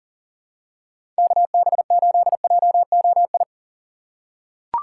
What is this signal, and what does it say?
Morse code, heard as a keyed tone, about 40 wpm, 700 Hz. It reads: KB8JOI
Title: USA Call Signs - 6 Characters - 3x Character Spacing - 40wpm